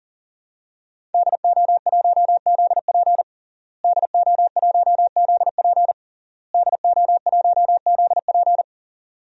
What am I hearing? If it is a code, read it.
Morse code, heard as a keyed tone, about 40 wpm, 700 Hz. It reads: DO1ZP DO1ZP DO1ZP
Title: DXCC-100 Call Signs - 5 Characters - Repeated 3x - 40wpm